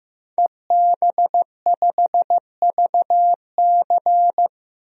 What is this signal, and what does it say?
Morse code, heard as a keyed tone, about 15 wpm, 700 Hz. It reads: EB5VC
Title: DXCC-100 Call Signs - 5 Characters - 15wpm